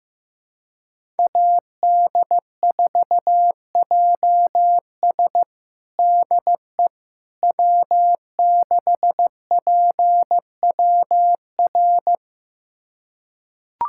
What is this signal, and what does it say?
Morse code, heard as a keyed tone, about 15 wpm, 700 Hz. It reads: AD4JS DE W6PWR
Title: USA Callsign DE Callsign - 5 Characters - 15wpm